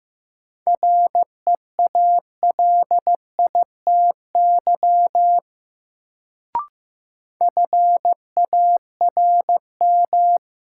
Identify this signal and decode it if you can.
Morse code, heard as a keyed tone, about 15 wpm, 700 Hz. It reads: REALITY FARM
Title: Sentences from Top 1200 Words - New Word Drill Alt - 15wpm